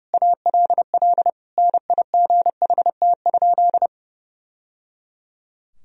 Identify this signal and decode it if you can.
Morse code, heard as a keyed tone, about 30 wpm, 700 Hz. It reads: ALL NIGHT?
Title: Sentences from Top 1000 Words - Review Part 4 - 30wpm